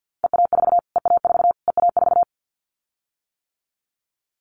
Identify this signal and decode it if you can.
Morse code, heard as a keyed tone, about 50 wpm, 700 Hz. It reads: EA4 EA4 EA4
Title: Callsign Prefixes for Top 25 DXCC - Repeated 3x - 50wpm